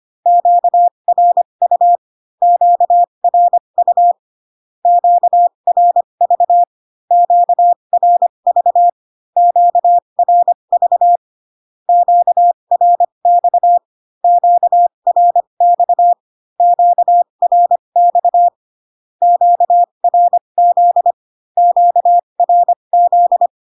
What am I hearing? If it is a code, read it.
Morse code, heard as a keyed tone, about 25 wpm, spaced as for 22 wpm, 700 Hz. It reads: QRU QRU QRV QRV QRV QRX QRX QRX QRZ QRZ
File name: Q-codes most common repeated 3 times 18 25wpm